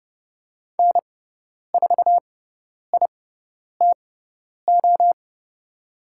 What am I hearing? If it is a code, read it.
Morse code, heard as a keyed tone, about 30 wpm, spaced as for 10 wpm, 700 Hz. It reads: N4ITO
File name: Lesson 18g - Letter m - Call Signs 30wpm